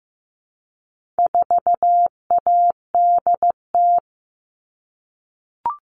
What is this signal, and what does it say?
Morse code, heard as a keyed tone, about 15 wpm, 700 Hz. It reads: 4ADT
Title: All Cut Numbers - 4-Digit 15wpm